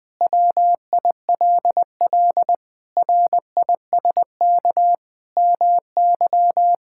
Morse code, heard as a keyed tone, about 20 wpm, 700 Hz. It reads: WILL RISK MY
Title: Sentences from Top 1000 Words - Review Part 5 - 20wpm